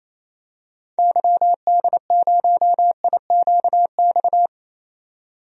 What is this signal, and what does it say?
Morse code, heard as a keyed tone, about 28 wpm, 700 Hz. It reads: YD0IQX
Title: DXCC-100 Call Signs - 6 Characters - 28wpm